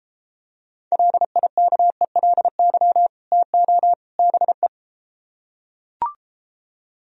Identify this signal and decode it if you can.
Morse code, heard as a keyed tone, about 33 wpm, 700 Hz. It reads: LIKELY TO BE